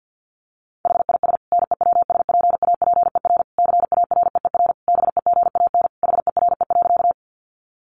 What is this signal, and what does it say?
Morse code, heard as a keyed tone, about 49 wpm, 700 Hz. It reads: HIS NEWSPAPER CAREER BEGAN HERE.
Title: Sentences from Top 1100 Words 50wpm